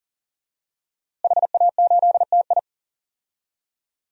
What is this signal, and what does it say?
Morse code, heard as a keyed tone, about 37 wpm, 700 Hz. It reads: HA8TI
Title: DXCC-100 Call Signs - 5 Characters - Repeated 3x - 40wpm